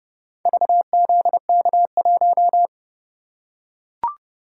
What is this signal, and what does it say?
Morse code, heard as a keyed tone, about 29 wpm, 700 Hz. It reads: VZK1